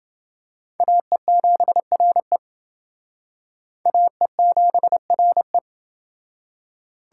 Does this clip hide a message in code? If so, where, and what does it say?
Yes: Morse code from 0.8 to 5.6 s, heard as a keyed tone, about 28 wpm, 700 Hz. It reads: AE7RE AE7RE